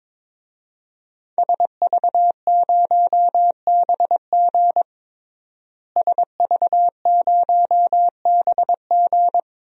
Morse code, heard as a keyed tone, about 22 wpm, 700 Hz. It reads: SV0BG SV0BG
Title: DXCC-100 Call Signs - 5 Characters - Repeated 3x - 22wpm